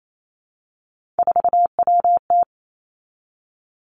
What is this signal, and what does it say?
Morse code, heard as a keyed tone, about 28 wpm, 700 Hz. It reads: 4WT
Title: Three Challenging Letter-Number Combinations 28wpm